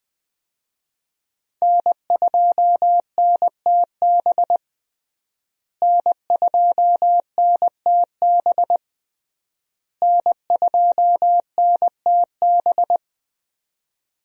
Morse code, heard as a keyed tone, about 20 wpm, 700 Hz. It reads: N2NTB N2NTB N2NTB
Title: USA Call Signs - 5 Characters - Repeated 3x - 20wpm